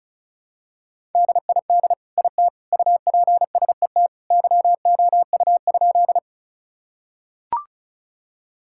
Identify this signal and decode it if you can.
Morse code, heard as a keyed tone, about 36 wpm, 700 Hz. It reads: DID IT UPSET YOU?
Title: Sentences from Top 1000 Words - Review Part 5 - 35wpm